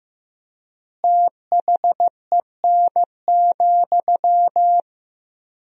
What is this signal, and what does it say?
Morse code, heard as a keyed tone, about 16 wpm, 700 Hz. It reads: THEN,
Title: 3x Word-Spacing - Sentences from Top 500 Words - 15wpm